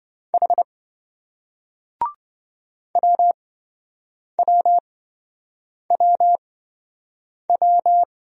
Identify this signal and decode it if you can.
Morse code, heard as a keyed tone, about 27 wpm, spaced as for 7 wpm, 700 Hz. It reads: H WWWW